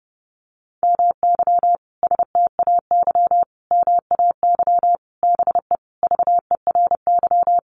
Morse code, heard as a keyed tone, about 30 wpm, 700 Hz. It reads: MY STAY MAY BE VERY
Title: Sentences from Top 1000 Words 30wpm